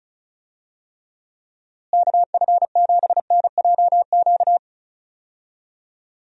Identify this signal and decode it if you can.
Morse code, heard as a keyed tone, about 34 wpm, 700 Hz. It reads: KF7NJQ
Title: USA Call Signs - 6 Characters - Repeated 3x - 35wpm